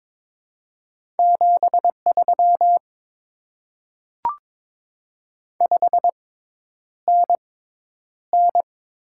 Morse code, heard as a keyed tone, about 22 wpm, spaced as for 10 wpm, 700 Hz. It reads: 73 5NN
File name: Top 20 QSO Elements - 6x Character Spacing - 22wpm